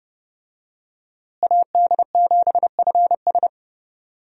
Morse code, heard as a keyed tone, about 30 wpm, 700 Hz. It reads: AD7FS
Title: USA Call Signs - 5 Characters - Repeated 2x - 30wpm